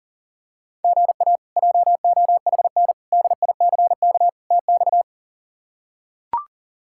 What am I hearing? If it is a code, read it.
Morse code, heard as a keyed tone, about 40 wpm, 700 Hz. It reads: GA JOHN DICK TX
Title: K1USN SST QSO Practice - 40wpm